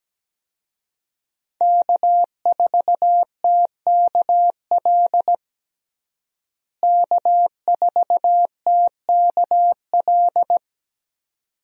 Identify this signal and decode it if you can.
Morse code, heard as a keyed tone, about 17 wpm, 700 Hz. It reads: K4TKL K4TKL
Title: USA Call Signs - 5 Characters - Repeated 2x - 17wpm